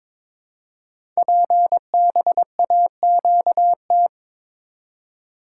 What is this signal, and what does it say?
Morse code, heard as a keyed tone, about 22 wpm, 700 Hz. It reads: PBAQT
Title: Code Groups - 5 Letters - 22wpm